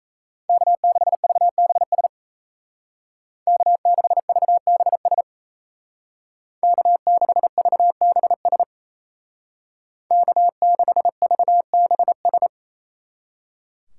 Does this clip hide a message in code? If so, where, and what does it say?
Yes: Morse code from 0.5 to 12.5 s, heard as a keyed tone, about 37 wpm, 700 Hz. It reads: K6VBS K6VBS K6VBS K6VBS